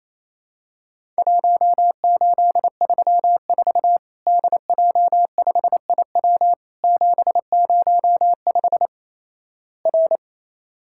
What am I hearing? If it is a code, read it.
Morse code, heard as a keyed tone, about 27 wpm, 700 Hz. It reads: 1834 DJ5IW 705 R
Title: Worked All Europe DX Contest - QTC Exchange - 28wpm